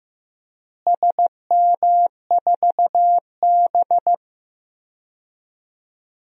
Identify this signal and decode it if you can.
Morse code, heard as a keyed tone, about 15 wpm, 700 Hz. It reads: SM4B
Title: DXCC-100 Call Signs - 4 Characters - Repeated 3x - 15wpm